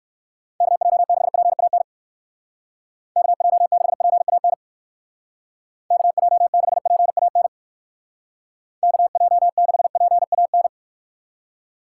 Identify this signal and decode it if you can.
Morse code, heard as a keyed tone, about 59 wpm, 700 Hz. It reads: KJ6PAN KJ6PAN KJ6PAN KJ6PAN